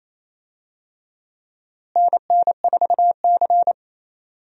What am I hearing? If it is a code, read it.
Morse code, heard as a keyed tone, about 28 wpm, 700 Hz. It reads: NN4C
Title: USA Call Signs - 4 Characters - Repeated 3x - 28wpm